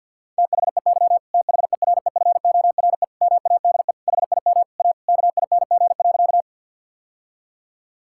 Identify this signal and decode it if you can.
Morse code, heard as a keyed tone, about 52 wpm, 690 Hz. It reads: THEY THEREFORE MADE HIM A KING.
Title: Sentences from Top 1000 Words - Review Part 5 - 50wpm